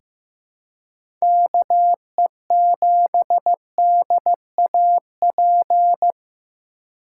Morse code, heard as a keyed tone, about 16 wpm, 700 Hz. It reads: KE7DAP
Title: USA Call Signs - 6 Characters - 2x Character Spacing - 15wpm